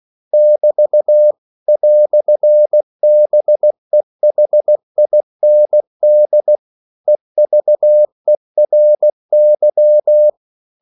Morse code, heard as a keyed tone, about 16 wpm, 600 Hz. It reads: <BT> "BEHIND EVERY